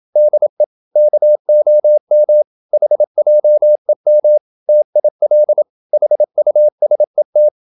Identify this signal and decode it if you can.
Morse code, heard as a keyed tone, about 27 wpm, 600 Hz. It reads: DE KOM HJEM TIL HUSET
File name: Ferie QRQ 27wpm